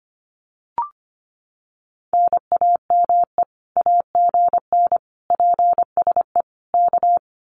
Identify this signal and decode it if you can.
Morse code, heard as a keyed tone, about 25 wpm, 700 Hz. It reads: NAME AGN PSE K
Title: Common QSO Phrases Encore 25wpm